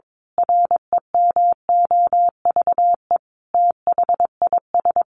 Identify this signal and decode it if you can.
Morse code, heard as a keyed tone, about 22 wpm, 700 Hz. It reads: REMOVE THIS
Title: Sentences from Top 1500 Words 22wpm